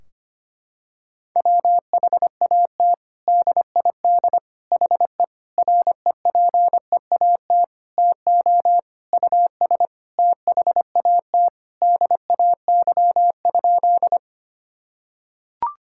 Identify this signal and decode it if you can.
Morse code, heard as a keyed tone, about 25 wpm, 700 Hz. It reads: WHAT DID HE REPEAT TO US THAT DAY?
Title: Sentences from Top 1600 Words 25wpm